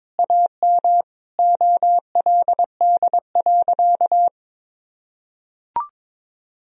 Morse code, heard as a keyed tone, about 21 wpm, 700 Hz. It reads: AM OLD.
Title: Sentences from Top 1000 Words - Review Part 3 - 22wpm